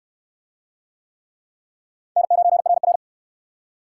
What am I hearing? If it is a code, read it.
Morse code, heard as a keyed tone, about 63 wpm, 700 Hz. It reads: N9RR